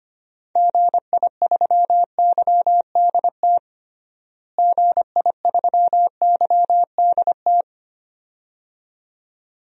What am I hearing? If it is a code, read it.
Morse code, heard as a keyed tone, about 25 wpm, 700 Hz. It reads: GI3YDT GI3YDT
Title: DXCC-100 Call Signs - 6 Characters - Repeated 2x - 25wpm